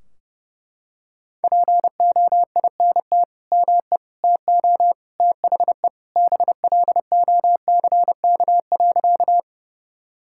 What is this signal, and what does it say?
Morse code, heard as a keyed tone, about 30 wpm, 700 Hz. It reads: POINT ME TO THE BLOCK.
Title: Sentences from Top 1600 Words 30wpm